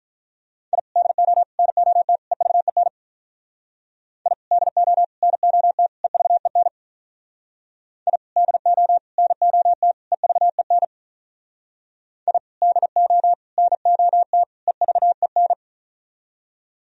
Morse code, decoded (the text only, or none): I DO NOT EVEN I DO NOT EVEN I DO NOT EVEN I DO NOT EVEN